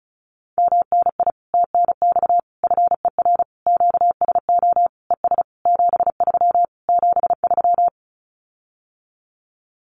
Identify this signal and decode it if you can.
Morse code, heard as a keyed tone, about 34 wpm, 700 Hz. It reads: MNI TNX FER QSO ES 73 73
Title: Common QSO Phrases Encore 35wpm